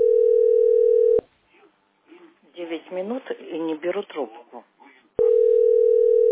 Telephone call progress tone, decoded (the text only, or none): ringback tone